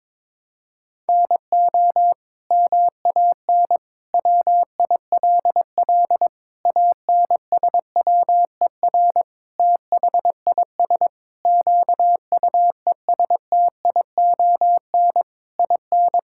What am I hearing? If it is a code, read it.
Morse code, heard as a keyed tone, about 22 wpm, 700 Hz. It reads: NO MAN WILL ANSWER THIS QUESTION IN
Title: Sentences from Top 1000 Words - Review Part 6 - 22wpm